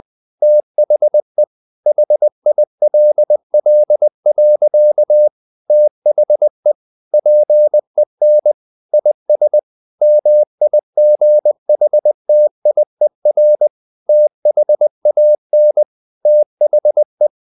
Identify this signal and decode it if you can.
Morse code, heard as a keyed tone, about 20 wpm, 600 Hz. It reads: THE HILL. THE PEN IS MIGHTIER THAN THE
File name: CW / Practice / EnglishSayings20